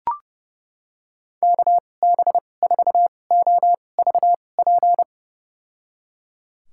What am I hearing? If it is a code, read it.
Morse code, heard as a keyed tone, about 32 wpm, spaced as for 22 wpm, 700 Hz. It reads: KB4OVP
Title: USA Call Signs - 6 Characters - 2x Character Spacing - 30wpm